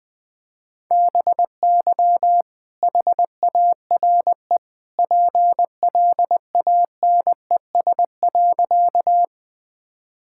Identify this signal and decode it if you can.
Morse code, heard as a keyed tone, about 20 wpm, 700 Hz. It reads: BY HARE PLANES.
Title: All-Age Appropriate Jokes - 20wpm